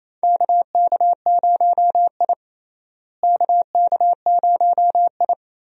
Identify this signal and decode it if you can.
Morse code, heard as a keyed tone, about 28 wpm, 700 Hz. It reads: KK0I KK0I
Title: USA Call Signs - 4 Characters - Repeated 3x - 28wpm